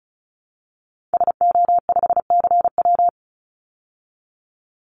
Morse code, heard as a keyed tone, about 34 wpm, 700 Hz. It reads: SO5CW
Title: Call Signs Repeated 2x - 35wpm